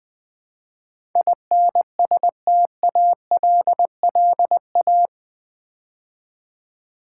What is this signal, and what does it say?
Morse code, heard as a keyed tone, about 20 wpm, 700 Hz. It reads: INSTALLA
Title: Build A Word - 4 Letters At A Time - 12-Letter Words - 20wpm